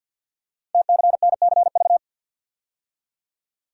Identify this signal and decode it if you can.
Morse code, heard as a keyed tone, about 52 wpm, 700 Hz. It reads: TCNCV
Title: Code Groups - 5 Letters - 50wpm